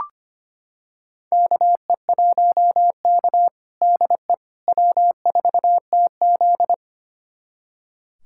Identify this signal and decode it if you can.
Morse code, heard as a keyed tone, about 25 wpm, 700 Hz. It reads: KE1K DE W4TZ
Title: USA Callsign DE Callsign - 4 Characters - 25wpm